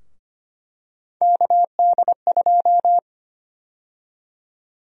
Morse code, heard as a keyed tone, about 25 wpm, 700 Hz. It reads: KD2
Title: USA Call Sign Prefixes - Repeated 2x - 25wpm